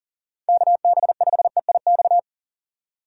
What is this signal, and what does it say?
Morse code, heard as a keyed tone, about 49 wpm, 700 Hz. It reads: KB5EIX
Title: Speed Racing - USA 2x3 Call Signs 40wpm